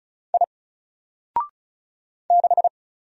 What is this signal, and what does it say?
Morse code, heard as a keyed tone, about 36 wpm, 700 Hz. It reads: I 6
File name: Single-Letter-Number-35wpm